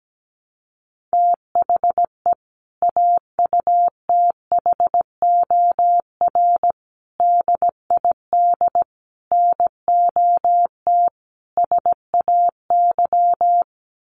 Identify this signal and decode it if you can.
Morse code, heard as a keyed tone, about 17 wpm, 700 Hz. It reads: THE AUTHOR DID NOT SAY
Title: Sentences from Top 1200 Words 17wpm